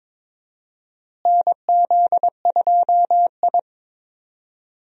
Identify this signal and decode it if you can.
Morse code, heard as a keyed tone, about 22 wpm, 700 Hz. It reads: NZ2I
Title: USA Call Signs - 4 Characters - Repeated 2x - 22wpm